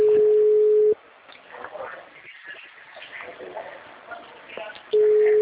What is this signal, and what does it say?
Ringback tone (Europe, 425 Hz).